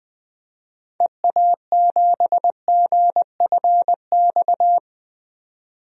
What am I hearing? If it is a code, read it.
Morse code, heard as a keyed tone, about 20 wpm, 700 Hz. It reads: EA7GFX
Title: DXCC-100 Call Signs - 6 Characters - 20wpm